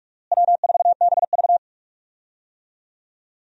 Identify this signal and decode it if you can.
Morse code, heard as a keyed tone, about 46 wpm, 700 Hz. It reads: W4BV
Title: USA Call Signs - 4 Characters - Repeated 2x - 45wpm